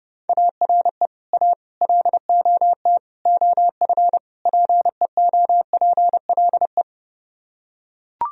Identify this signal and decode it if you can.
Morse code, heard as a keyed tone, about 31 wpm, 700 Hz. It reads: ARE A LOT OF PEOPLE